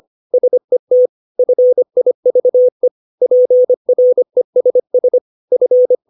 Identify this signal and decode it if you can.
Morse code, heard as a keyed tone, about 25 wpm, 500 Hz. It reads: SET FIVE PRESS F
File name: 632 words / 632 Words 25WPM